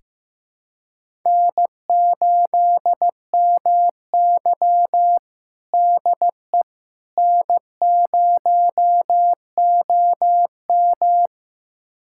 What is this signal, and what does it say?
Morse code, heard as a keyed tone, about 15 wpm, 700 Hz. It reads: N8MY DE N0OM
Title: USA Callsign DE Callsign - 4 Characters - 15wpm